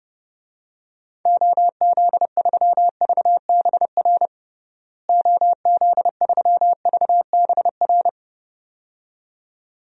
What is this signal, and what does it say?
Morse code, heard as a keyed tone, about 32 wpm, 700 Hz. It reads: OZ3VBR OZ3VBR
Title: DXCC-100 Call Signs - 6 Characters - Repeated 2x - 30wpm